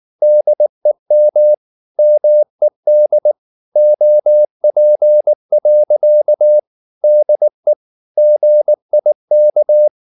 Morse code, heard as a keyed tone, about 19 wpm, 600 Hz. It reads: DEM MED OP. DE GIK
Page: Ferie 19 wpm